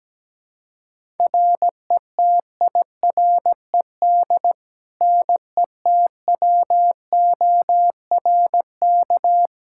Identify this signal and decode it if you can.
Morse code, heard as a keyed tone, about 17 wpm, 700 Hz. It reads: RETIRED NETWORK
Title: Practice QSO - Exchange 3 - More Info - 17wpm